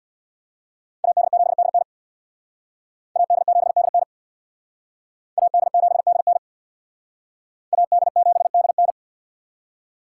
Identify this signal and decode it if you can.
Morse code, heard as a keyed tone, about 62 wpm, 700 Hz. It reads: AD7DN AD7DN AD7DN AD7DN
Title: Speed Racing - USA 2x2 Call Signs 50wpm